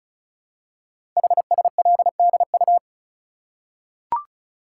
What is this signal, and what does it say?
Morse code, heard as a keyed tone, about 36 wpm, 700 Hz. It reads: HSLDU
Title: Code Groups - 5 Letter:Numbers - 35wpm